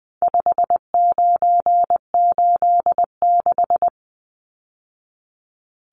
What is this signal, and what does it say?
Morse code, heard as a keyed tone, about 20 wpm, 700 Hz. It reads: 5986
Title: Numbers - 4-Digit 20wpm